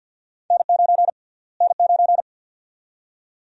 Morse code, heard as a keyed tone, about 49 wpm, 700 Hz. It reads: N9 N9
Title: USA Call Sign Prefixes - Repeated 2x - 50wpm